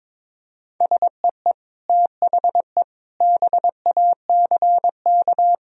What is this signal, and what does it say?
Morse code, heard as a keyed tone, about 22 wpm, 700 Hz. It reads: SEE THE BACK
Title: Sentences from Top 1000 Words - Review Part 1 - 22wpm